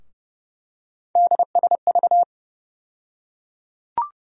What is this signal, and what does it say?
Morse code, heard as a keyed tone, about 30 wpm, 700 Hz. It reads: DSV
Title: USA Call Sign Postfixes - 3 Letters 30wpm